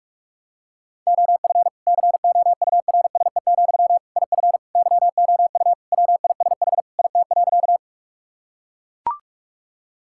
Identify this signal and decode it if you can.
Morse code, heard as a keyed tone, about 41 wpm, 700 Hz. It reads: OF COURSE, IF YOU WISH IT.
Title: Sentences from Top 1600 Words 45wpm